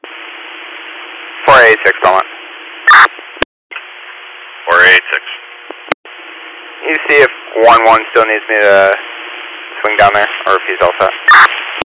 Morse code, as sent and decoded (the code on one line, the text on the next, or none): none